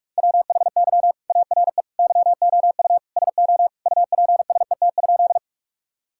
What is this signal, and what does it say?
Morse code, heard as a keyed tone, about 46 wpm, 700 Hz. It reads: WHY ARE YOU SO UPSET?
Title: Sentences from Top 1000 Words - Review Part 5 - 45wpm